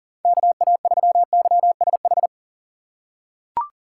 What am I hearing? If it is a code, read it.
Morse code, heard as a keyed tone, about 40 wpm, 700 Hz. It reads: KA3YSH
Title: USA Call Signs - 2x Character Spacing - 40wpm